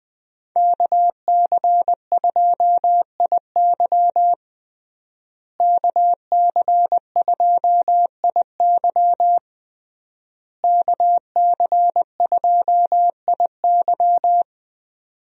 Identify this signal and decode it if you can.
Morse code, heard as a keyed tone, about 20 wpm, 700 Hz. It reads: KC2IY KC2IY KC2IY
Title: USA Call Signs - 5 Characters - Repeated 3x - 20wpm